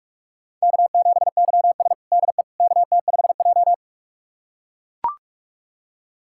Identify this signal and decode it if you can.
Morse code, heard as a keyed tone, about 43 wpm, 700 Hz. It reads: K7YS DE KT5J
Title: USA Callsign DE Callsign - 4 Characters - 45wpm